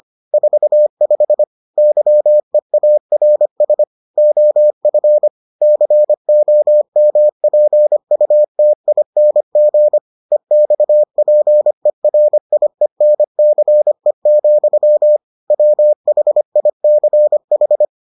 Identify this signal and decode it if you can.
Morse code, heard as a keyed tone, about 25 wpm, 600 Hz. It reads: 45 YEARS OF COMPUTING EXPERIENCE, WHICH